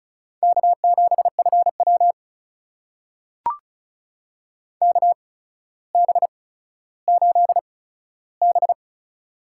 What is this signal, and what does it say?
Morse code, heard as a keyed tone, about 36 wpm, spaced as for 15 wpm, 700 Hz. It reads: K7FW KB8B
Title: USA Call Signs - 4 Characters - 8x Character Spacing - 35wpm